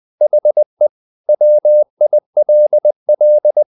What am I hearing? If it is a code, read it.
Morse code, heard as a keyed tone, about 20 wpm, 600 Hz. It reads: HE WILL
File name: the old man and the sea / the old man and the sea part 1 20wpm